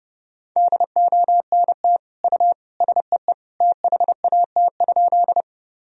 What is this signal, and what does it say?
Morse code, heard as a keyed tone, about 32 wpm, 700 Hz. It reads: DONT U SEE THAT?
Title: Sentences from Top 300 Words and Abbreviations 30wpm